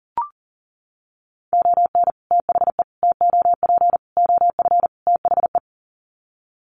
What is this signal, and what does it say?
Morse code, heard as a keyed tone, about 40 wpm, 700 Hz. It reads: ON THE TOP OF THE
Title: Sets of 5 Words Encore 40wpm